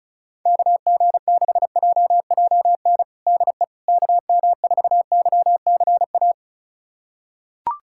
Morse code, heard as a keyed tone, about 34 wpm, 700 Hz. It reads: KG6JJN DE KM4YCA
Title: USA Callsign DE Callsign - 6 Characters - 35wpm